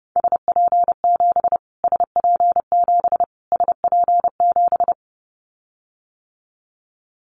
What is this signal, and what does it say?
Morse code, heard as a keyed tone, about 30 wpm, 700 Hz. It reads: SP7 SP7 SP7
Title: Callsign Prefixes for Top 25 DXCC - Repeated 3x - 30wpm